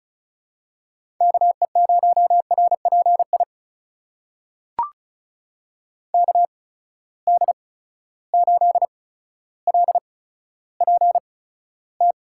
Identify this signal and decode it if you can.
Morse code, heard as a keyed tone, about 36 wpm, spaced as for 15 wpm, 700 Hz. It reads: KE0RPI KD8LPT
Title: USA Call Signs - 6 Characters - 8x Character Spacing - 35wpm